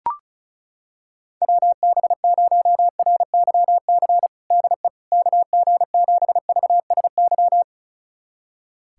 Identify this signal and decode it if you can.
Morse code, heard as a keyed tone, about 36 wpm, 700 Hz. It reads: WB0RYC DE KG7VSY
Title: USA Callsign DE Callsign 35wpm